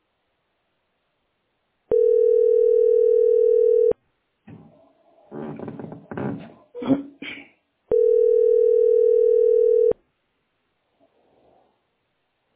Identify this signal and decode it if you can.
Ringback tone (North America), 440 + 480 Hz.